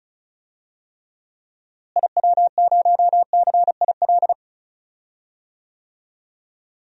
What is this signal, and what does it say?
Morse code, heard as a keyed tone, about 36 wpm, 700 Hz. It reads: IW0CIL
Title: DXCC-100 Call Signs - 6 Characters - Repeated 2x - 35wpm